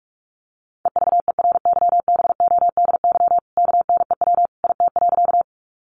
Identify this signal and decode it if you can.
Morse code, heard as a keyed tone, about 44 wpm, 700 Hz. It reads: EVERYBODY KNEW IT.
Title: Sentences from Top 1400 Words 45wpm